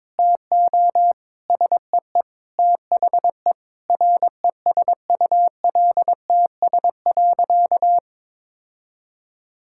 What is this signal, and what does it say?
Morse code, heard as a keyed tone, about 21 wpm, 700 Hz. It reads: TO SEE THE RESULTS.
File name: Sentences with 7-Word Phrases - 22wpm